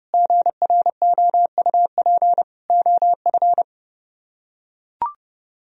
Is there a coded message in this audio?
Yes: Morse code from 0.1 to 3.6 s, heard as a keyed tone, about 28 wpm, 700 Hz. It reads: GROUP OF